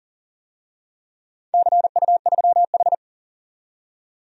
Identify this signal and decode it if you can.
Morse code, heard as a keyed tone, about 40 wpm, 700 Hz. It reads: CU3H
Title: DXCC-100 Call Signs - 4 Characters - 40wpm